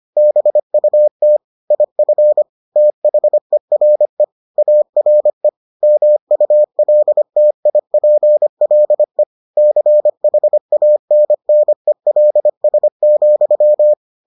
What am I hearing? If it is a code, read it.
Morse code, heard as a keyed tone, about 25 wpm, 600 Hz. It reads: BUT IF THERE ARE MULTIPLE CHANNELS,